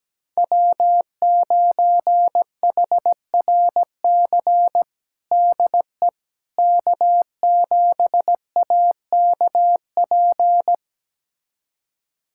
Morse code, heard as a keyed tone, about 17 wpm, 700 Hz. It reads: W9HRC DE K7AKP
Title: USA Callsign DE Callsign - 5 Characters - 17wpm